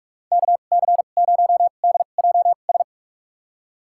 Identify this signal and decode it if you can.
Morse code, heard as a keyed tone, about 46 wpm, spaced as for 33 wpm, 700 Hz. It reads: KC0DJS
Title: USA Call Signs - 6 Characters - 2x Character Spacing - 45wpm